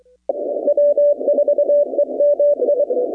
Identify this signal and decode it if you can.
Morse code, heard as a keyed tone, about 23 wpm, 580 Hz. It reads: W4EMI
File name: contests / 2019-cw-wpx-kt5j-k5pi- -2330 0-39